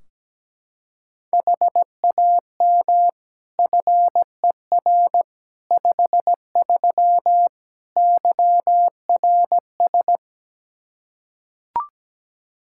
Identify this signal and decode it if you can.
Morse code, heard as a keyed tone, about 17 wpm, 700 Hz. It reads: HAM FER 53 YRS
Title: Practice QSO - Exchange 3 - More Info - 17wpm